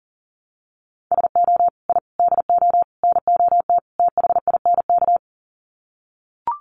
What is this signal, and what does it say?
Morse code, heard as a keyed tone, about 40 wpm, 700 Hz. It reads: SO I DO NOT THINK
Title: Sets of 5 Words 40wpm